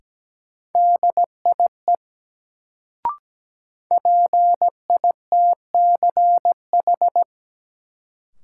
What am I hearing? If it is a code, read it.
Morse code, heard as a keyed tone, about 17 wpm, 700 Hz. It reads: DIE PITCH
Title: Sentences from Top 1300 Words - New Word Drill - 17wpm